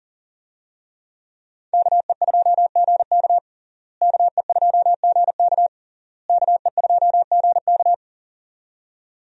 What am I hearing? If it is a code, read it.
Morse code, heard as a keyed tone, about 37 wpm, 700 Hz. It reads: KE2GK KE2GK KE2GK
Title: USA Call Signs - 5 Characters - Repeated 3x - 40wpm